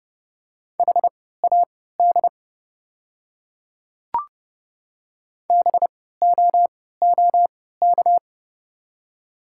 Morse code, heard as a keyed tone, about 30 wpm, spaced as for 17 wpm, 700 Hz. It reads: HAD BOOK
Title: 3x Character Spacing - Top 300 Words - 30wpm